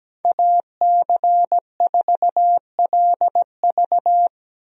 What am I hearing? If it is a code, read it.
Morse code, heard as a keyed tone, about 17 wpm, 700 Hz. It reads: AC4LV
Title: USA Call Signs - 5 Characters - Repeated 2x - 17wpm